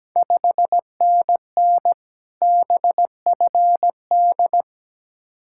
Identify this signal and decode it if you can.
Morse code, heard as a keyed tone, about 17 wpm, 700 Hz. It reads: 5NN BFD
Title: Ontario QSO Party Exchange 17wpm